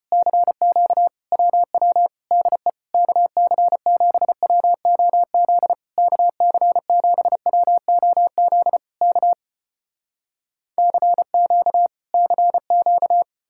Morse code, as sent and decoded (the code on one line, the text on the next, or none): -.-. --.- / .-- .-- / -.. / -.- -.-. --... .-- --- --.. / -.- -.-. --... .-- --- --.. / -.- / -.-. --.- / -.-. --.-
CQ WW DE KC7WOZ KC7WOZ K CQ CQ